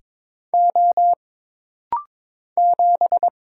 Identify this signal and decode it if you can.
Morse code, heard as a keyed tone, about 22 wpm, 700 Hz. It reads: O 7
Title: Single Letter-Number Alt - Rapid-Fire 22wpm